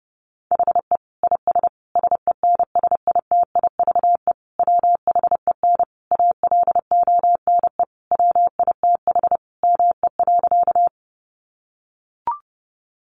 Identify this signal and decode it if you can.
Morse code, heard as a keyed tone, about 30 wpm, 700 Hz. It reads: HE IS SENSITIVE WHEN ALONE WITH ME.
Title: Sentences from Top 1400 Words 30wpm